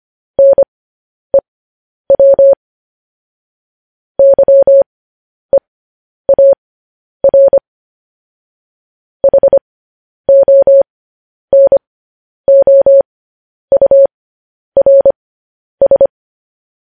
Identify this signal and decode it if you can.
Morse code, heard as a keyed tone, about 25 wpm, spaced as for 10 wpm, 550 Hz. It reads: NEW YEAR HONOURS
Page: News Headlines in Morse Code at 10 WPM